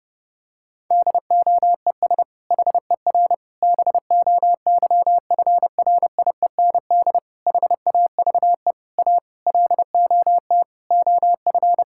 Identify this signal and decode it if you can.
Morse code, heard as a keyed tone, about 30 wpm, 700 Hz. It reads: DOES HER BOYFRIEND HAVE A LOT OF
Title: Sentences from Top 1000 Words - Review Part 5 - 30wpm